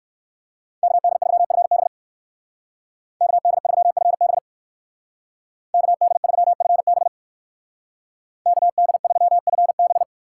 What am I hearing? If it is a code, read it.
Morse code, heard as a keyed tone, about 59 wpm, 700 Hz. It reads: KD3FB KD3FB KD3FB KD3FB